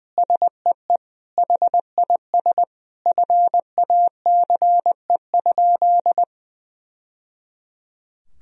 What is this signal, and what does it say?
Morse code, heard as a keyed tone, about 20 wpm, 700 Hz. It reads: SEE HIS FACE?
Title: Sentences from Top 1000 Words - Review Part 2 - 20wpm